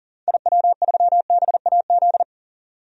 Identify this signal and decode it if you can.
Morse code, heard as a keyed tone, about 40 wpm, 700 Hz. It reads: IW3BAZ
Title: DXCC-100 Call Signs - 6 Characters - 40wpm